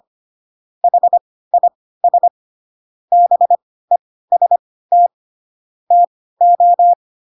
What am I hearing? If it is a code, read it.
Morse code, heard as a keyed tone, about 25 wpm, spaced as for 16 wpm, 700 Hz. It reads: HIS BEST TO
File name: Pinocchio / Pinocchio ch3 10 25 ik1hge